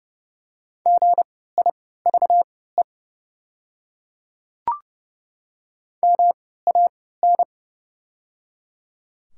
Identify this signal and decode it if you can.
Morse code, heard as a keyed tone, about 30 wpm, spaced as for 17 wpm, 700 Hz. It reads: GIVE MAN
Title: 3x Character Spacing - Top 200 Words - 30wpm